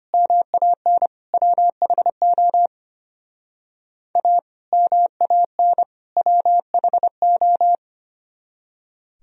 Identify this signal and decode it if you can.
Morse code, heard as a keyed tone, about 30 wpm, 700 Hz. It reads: MAN WHO A MAN WHO
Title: Speed-Racing - Sets of 3 Words 25wpm